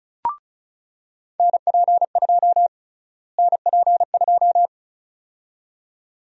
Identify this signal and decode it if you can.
Morse code, heard as a keyed tone, about 36 wpm, 700 Hz. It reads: NP2 NP2
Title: USA Call Sign Prefixes - Repeated 2x - 35wpm